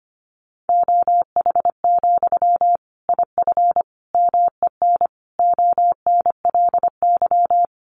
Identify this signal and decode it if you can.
Morse code, heard as a keyed tone, about 25 wpm, 700 Hz. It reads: OH, IF MEN ONLY
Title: Sentences from Top 500 Words 25wpm